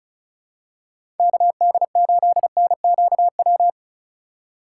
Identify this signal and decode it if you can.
Morse code, heard as a keyed tone, about 34 wpm, 700 Hz. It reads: KD8NQW
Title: USA Call Signs - 6 Characters - 4x Character Spacing - 35wpm